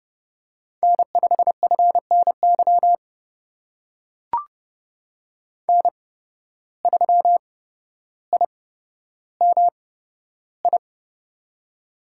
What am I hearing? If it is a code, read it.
Morse code, heard as a keyed tone, about 30 wpm, spaced as for 13 wpm, 700 Hz. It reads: N5FNY N3IMI
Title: USA Call Signs - 5 Characters - 8x Character Spacing - 30wpm